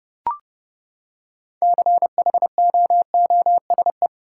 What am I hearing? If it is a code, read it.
Morse code, heard as a keyed tone, about 30 wpm, 700 Hz. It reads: CHOOSE
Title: Six Letter Words - 30wpm